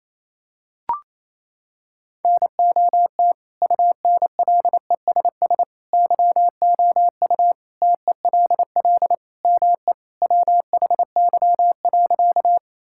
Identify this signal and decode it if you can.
Morse code, heard as a keyed tone, about 28 wpm, 700 Hz. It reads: NOT UNLESS YOU TELL ME WHY.
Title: Sentences from Top 1000 Words - Review Part 6 - 28wpm